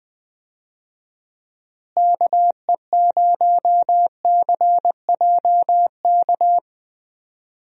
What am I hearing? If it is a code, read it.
Morse code, heard as a keyed tone, about 20 wpm, 700 Hz. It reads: KE0CJK
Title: USA Call Signs - 6 Characters - Repeated 2x - 20wpm